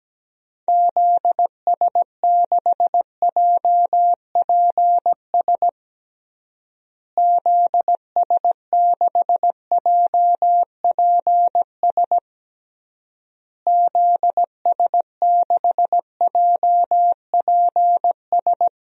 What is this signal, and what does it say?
Morse code, heard as a keyed tone, about 17 wpm, 700 Hz. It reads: ZS6JPS ZS6JPS ZS6JPS
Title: DXCC-100 Call Signs - 6 Characters - Repeated 3x - 17wpm